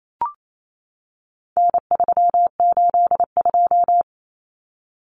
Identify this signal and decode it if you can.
Morse code, heard as a keyed tone, about 28 wpm, 700 Hz. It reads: N382
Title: Common Cut Numbers - 4-Digit 28wpm